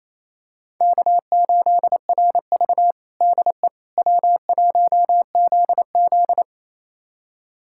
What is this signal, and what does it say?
Morse code, heard as a keyed tone, about 28 wpm, 700 Hz. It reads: K8RV DE W1ZZ
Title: USA Callsign DE Callsign - 4 Characters - 28wpm